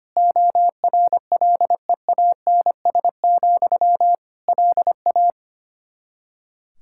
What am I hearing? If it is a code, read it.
Morse code, heard as a keyed tone, about 25 wpm, 700 Hz. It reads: ORLEANS, LA
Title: Practice QSO - Exchange 1 - Beginning - 25wpm